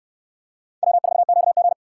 Morse code, heard as a keyed tone, about 68 wpm, 700 Hz. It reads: W4QG